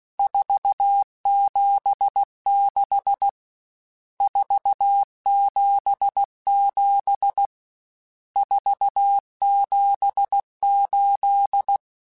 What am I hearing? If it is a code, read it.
Morse code, heard as a keyed tone, about 16 wpm, 800 Hz. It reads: 476 477 478
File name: RSTs / RST-mp3s / Various RST @15